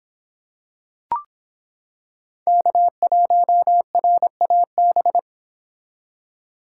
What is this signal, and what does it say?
Morse code, heard as a keyed tone, about 26 wpm, 700 Hz. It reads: K1RAB